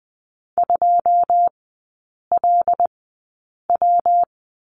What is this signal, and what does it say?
Morse code, heard as a keyed tone, about 20 wpm, spaced as for 8 wpm, 700 Hz. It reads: 2LW
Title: 114 Lesson 19e - Letter w - Three Characters 20wpm